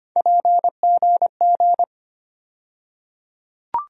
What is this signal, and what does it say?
Morse code, heard as a keyed tone, about 25 wpm, 700 Hz. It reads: PGG
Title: USA Call Sign Postfixes - 3 Letters 25wpm